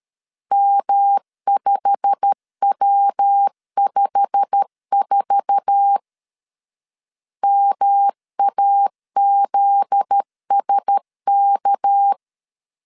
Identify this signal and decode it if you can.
Morse code, heard as a keyed tone, about 13 wpm, 800 Hz. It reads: M5W54 MAZSK
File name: CW / DET / 1 9 1500C S B 12wpm 3819